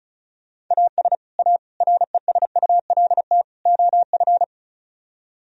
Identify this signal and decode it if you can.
Morse code, heard as a keyed tone, about 43 wpm, 700 Hz. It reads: AS A RESULT OF
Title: Speed-Racing - Sets of 4 Words 35wpm